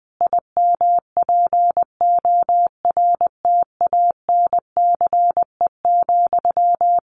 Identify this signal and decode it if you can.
Morse code, heard as a keyed tone, about 20 wpm, 700 Hz. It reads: IMPORTANCE,
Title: Sentences from Top 1500 Words 20wpm